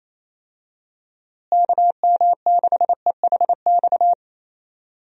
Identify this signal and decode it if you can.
Morse code, heard as a keyed tone, about 28 wpm, 700 Hz. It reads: KM6EHX
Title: USA Call Signs - 6 Characters - Repeated 2x - 28wpm